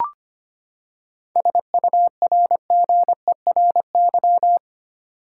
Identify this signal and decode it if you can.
Morse code, heard as a keyed tone, about 25 wpm, 700 Hz. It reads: SURGERY
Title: Seven Letter Words - 25wpm